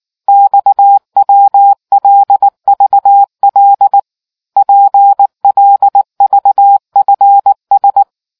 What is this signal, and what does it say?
Morse code, heard as a keyed tone, about 19 wpm, 800 Hz. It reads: XWLVL PLVFS